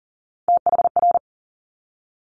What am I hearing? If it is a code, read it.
Morse code, heard as a keyed tone, about 40 wpm, 700 Hz. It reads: THR
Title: Top Three Letter Combinations in Words 40wpm